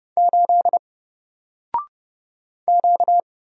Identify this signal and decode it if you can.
Morse code, heard as a keyed tone, about 30 wpm, 700 Hz. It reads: 8 Q
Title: Single Letter-Number Alt 30wpm